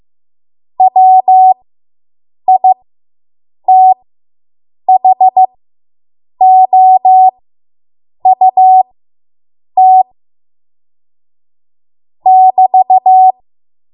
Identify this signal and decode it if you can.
Morse code, heard as a keyed tone, about 15 wpm, spaced as for 7 wpm, 750 Hz. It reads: WITHOUT <BT>